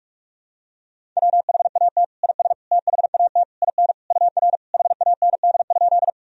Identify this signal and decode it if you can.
Morse code, heard as a keyed tone, about 46 wpm, 700 Hz. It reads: WHAT IS THAT IN UR HAND?
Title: Sentences from Top 200 Words and Abbreviations 45wpm